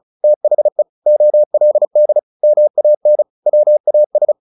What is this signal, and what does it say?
Morse code, heard as a keyed tone, about 34 wpm, 600 Hz. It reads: THE OLD MAN WAS
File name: the old man and the sea / the old man and the sea 1 35wpm